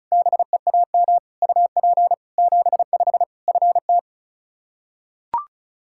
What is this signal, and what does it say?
Morse code, heard as a keyed tone, about 34 wpm, 700 Hz. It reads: BEAM UP 75 FT
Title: K1USN SST QSO Practice 35wpm